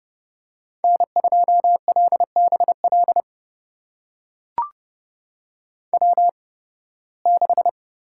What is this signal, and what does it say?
Morse code, heard as a keyed tone, about 30 wpm, spaced as for 20 wpm, 700 Hz. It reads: N2LBL W6
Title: USA Call Signs - 8x Character Spacing - 30wpm